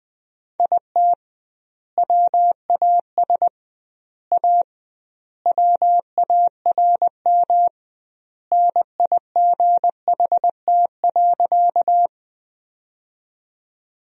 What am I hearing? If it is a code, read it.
Morse code, heard as a keyed tone, about 21 wpm, 700 Hz. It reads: IT WAS A WARM NIGHT.
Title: 2x Word-Spacing - Sentences from Top 500 Words - 20wpm